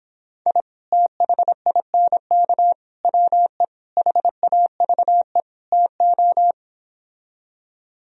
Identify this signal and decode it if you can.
Morse code, heard as a keyed tone, about 26 wpm, 700 Hz. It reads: I THINK WE HAVE TO